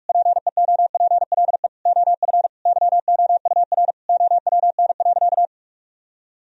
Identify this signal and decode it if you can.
Morse code, heard as a keyed tone, about 46 wpm, 700 Hz. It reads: PEOPLE OF YOUR OWN.
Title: Sentences from Top 1000 Words - Review Part 2 - 45wpm